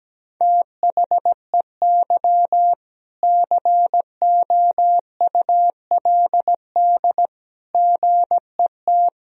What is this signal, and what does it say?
Morse code, heard as a keyed tone, about 17 wpm, 700 Hz. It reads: THEY COULD GET
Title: Sentences from Top 1000 Words - Review Part 3 - 17wpm